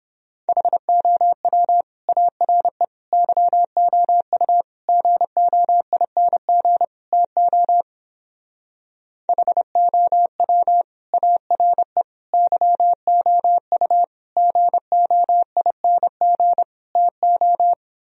Morse code, heard as a keyed tone, about 31 wpm, 700 Hz. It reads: HOW ARE YOU GOING TO HOW ARE YOU GOING TO